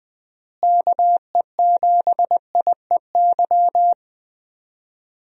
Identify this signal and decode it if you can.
Morse code, heard as a keyed tone, about 20 wpm, 700 Hz. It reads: KE7IEY
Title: USA Call Signs - 6 Characters - Repeated 3x - 20wpm